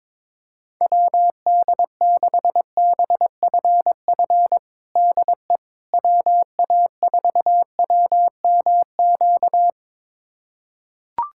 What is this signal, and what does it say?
Morse code, heard as a keyed tone, about 22 wpm, 700 Hz. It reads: WD6BFF DE WA4WMQ
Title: USA Callsign DE Callsign - 6 Characters - 22wpm